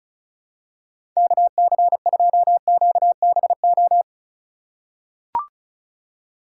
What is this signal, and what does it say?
Morse code, heard as a keyed tone, about 36 wpm, 700 Hz. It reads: KC2QBO
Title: USA Call Signs - 6 Characters - 4x Character Spacing - 35wpm